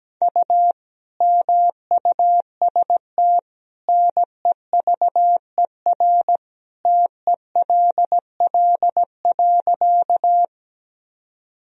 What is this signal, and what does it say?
Morse code, heard as a keyed tone, about 17 wpm, 700 Hz. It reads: U MUST NEVER TELL.
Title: Sentences from Top 300 Words and Abbreviations 17wpm